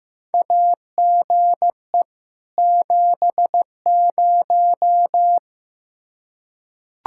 Morse code, heard as a keyed tone, about 15 wpm, 700 Hz. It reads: AGE 70
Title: Practice QSO - Exchange 3 - More Info - 15wpm